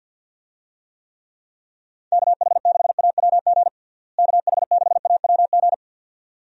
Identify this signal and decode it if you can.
Morse code, heard as a keyed tone, about 51 wpm, 700 Hz. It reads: KH6AWG KH6AWG
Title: DXCC-100 Call Signs - 6 Characters - Repeated 2x - 50wpm